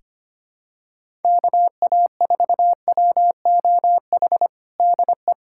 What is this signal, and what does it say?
Morse code, heard as a keyed tone, about 25 wpm, 700 Hz. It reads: KA4WOH DE
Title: USA Callsign DE Callsign 25wpm